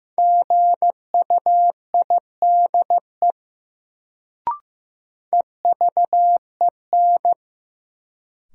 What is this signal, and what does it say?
Morse code, heard as a keyed tone, about 15 wpm, 700 Hz. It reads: GUIDE EVEN
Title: Sentences from Top 600 Words - New Word Drill - 15wpm